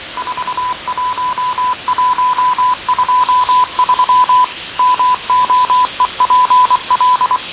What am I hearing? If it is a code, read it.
Morse code, heard as a keyed tone, about 24 wpm, 1000 Hz. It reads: 41123 MOEPL
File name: fav22-6824usb-20251029-1420z-voice-test-win10